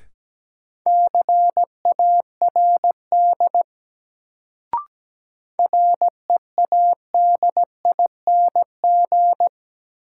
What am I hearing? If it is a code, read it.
Morse code, heard as a keyed tone, about 17 wpm, 700 Hz. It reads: CARD READING
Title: Sentences from Top 600 Words - New Word Drill - 17wpm